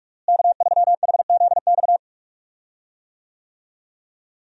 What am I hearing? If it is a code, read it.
Morse code, heard as a keyed tone, about 46 wpm, 700 Hz. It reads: K3HZX
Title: USA Call Signs - 5 Characters - Repeated 2x - 45wpm